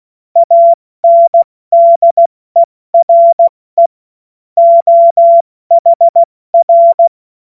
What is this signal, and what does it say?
Morse code, heard as a keyed tone, about 16 wpm, 680 Hz. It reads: ANDERE OHR
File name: Witz-09-September-2025-70-F80